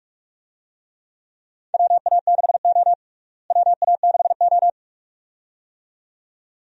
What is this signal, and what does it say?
Morse code, heard as a keyed tone, about 47 wpm, 700 Hz. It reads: WA6O WA6O
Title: USA Call Signs - 4 Characters - Repeated 2x - 45wpm